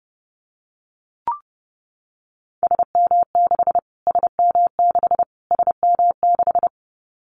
Morse code, heard as a keyed tone, about 30 wpm, 700 Hz. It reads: SM6 SM6 SM6
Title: Callsign Prefixes for Top 25 DXCC - Repeated 3x - 30wpm